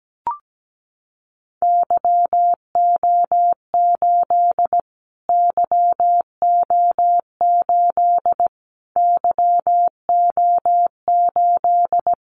Morse code, heard as a keyed tone, about 17 wpm, 700 Hz. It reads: YO8 YO8 YO8
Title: Callsign Prefixes for Top 25 DXCC - Repeated 3x - 17wpm